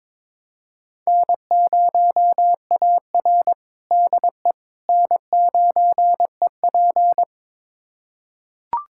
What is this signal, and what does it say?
Morse code, heard as a keyed tone, about 22 wpm, 700 Hz. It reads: N0AR DE N9EP
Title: USA Callsign DE Callsign - 4 Characters - 22wpm